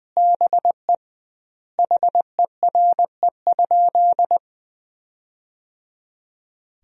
Morse code, heard as a keyed tone, about 20 wpm, 700 Hz. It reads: BE HERE?
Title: 2x Word-Spacing - Sentences from Top 400 Words - 20wpm